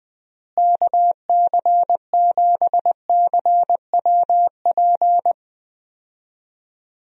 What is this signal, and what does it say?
Morse code, heard as a keyed tone, about 20 wpm, 700 Hz. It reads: KC7CWP
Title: USA Call Signs - 6 Characters - Repeated 3x - 20wpm